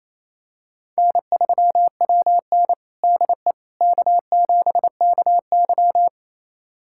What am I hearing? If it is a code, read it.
Morse code, heard as a keyed tone, about 28 wpm, 700 Hz. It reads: N3WN DE K7KY
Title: USA Callsign DE Callsign - 4 Characters - 28wpm